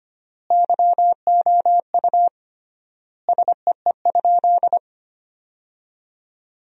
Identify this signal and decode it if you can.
Morse code, heard as a keyed tone, about 25 wpm, 700 Hz. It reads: YOU SEE?
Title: 3x Word-Spacing - Sentences from Top 100 Words - 25wpm